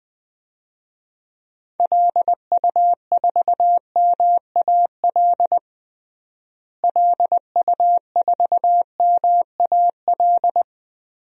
Morse code, heard as a keyed tone, about 20 wpm, 700 Hz. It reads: LU4MAL LU4MAL
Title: DXCC-100 Call Signs - 6 Characters - Repeated 3x - 20wpm